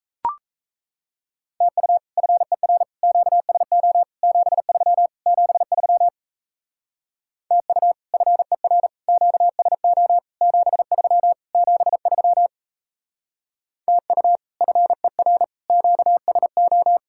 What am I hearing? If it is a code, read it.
Morse code, heard as a keyed tone, about 35 wpm, 700 Hz. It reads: TU FER QSO 73 73 TU FER QSO 73 73 TU FER QSO